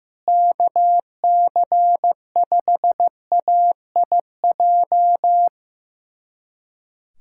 Morse code, heard as a keyed tone, about 15 wpm, 700 Hz. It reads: KC5AIJ
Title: USA Callsign DE Callsign - 6 Characters - 15wpm